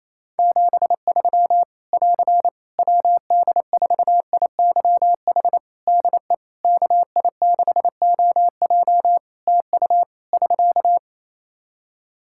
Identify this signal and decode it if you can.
Morse code, heard as a keyed tone, about 27 wpm, 700 Hz. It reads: 73 <AR> WD4IYH DE KI6OJ TU <SK>
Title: K1USN SST QSO Practice 28wpm